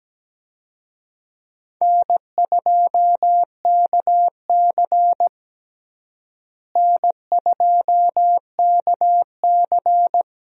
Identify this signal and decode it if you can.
Morse code, heard as a keyed tone, about 17 wpm, 700 Hz. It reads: N2KC N2KC
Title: USA Call Signs - 4 Characters - Repeated 3x - 17wpm